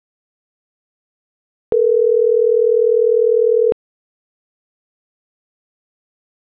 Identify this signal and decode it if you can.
Ringback tone (North America), 440 + 480 Hz.